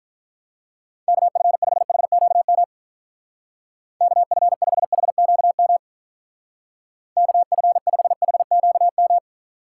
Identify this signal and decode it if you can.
Morse code, heard as a keyed tone, about 44 wpm, 700 Hz. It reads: KF5HQM KF5HQM KF5HQM